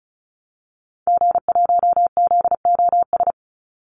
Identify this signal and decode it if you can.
Morse code, heard as a keyed tone, about 34 wpm, 700 Hz. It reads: G1ZOS
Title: Call Signs Repeated 2x - 35wpm